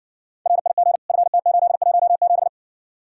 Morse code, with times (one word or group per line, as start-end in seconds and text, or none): 0.5-1.0 RIG
1.1-2.5 FT817